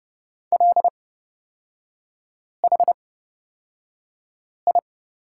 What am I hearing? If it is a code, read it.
Morse code, heard as a keyed tone, about 30 wpm, spaced as for 5 wpm, 700 Hz. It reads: LHI
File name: Lesson 37e - Letter x - Three Characters 30wpm